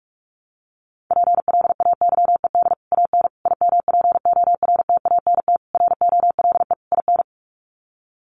Morse code, heard as a keyed tone, about 44 wpm, 690 Hz. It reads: PLAYED AN IMPORTANT ROLE IN
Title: Sets of 5 Words Encore2 45wpm